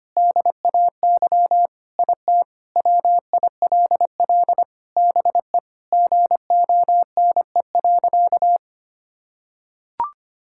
Morse code, heard as a keyed tone, about 25 wpm, 700 Hz. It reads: DAY IT WILL BE GONE.
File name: practice / 25wpm / Sentences with 4-Word Phrases - 25wpm